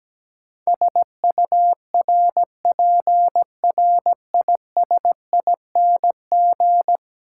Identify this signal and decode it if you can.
Morse code, heard as a keyed tone, about 17 wpm, 700 Hz. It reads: SURPRISING
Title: Sets of 5 Words - New Word Drill - 17wpm